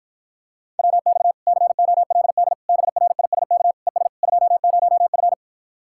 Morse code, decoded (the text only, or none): WX COLD BRISK ES 20F